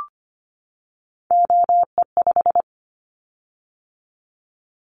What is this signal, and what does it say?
Morse code, heard as a keyed tone, about 25 wpm, 700 Hz. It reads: OE5
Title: Callsign Prefixes for Top 50 DXCC - 25wpm